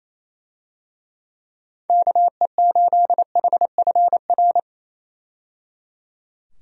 Morse code, heard as a keyed tone, about 28 wpm, 700 Hz. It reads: KE8HFR
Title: USA Call Signs - 6 Characters - 28wpm